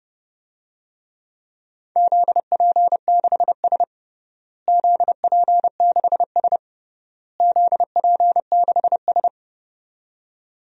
Morse code, heard as a keyed tone, about 28 wpm, 700 Hz. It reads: ZP6S ZP6S ZP6S
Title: DXCC-100 Call Signs - 4 Characters - Repeated 3x - 30wpm